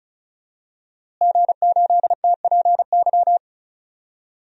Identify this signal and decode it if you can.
Morse code, heard as a keyed tone, about 36 wpm, 700 Hz. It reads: G8TPY
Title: DXCC-100 Call Signs - 5 Characters - Repeated 3x - 35wpm